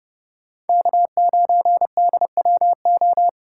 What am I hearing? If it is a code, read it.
Morse code, heard as a keyed tone, about 37 wpm, 700 Hz. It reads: K9DWO
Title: Speed-Racing Common QSO Phrases - 30wpm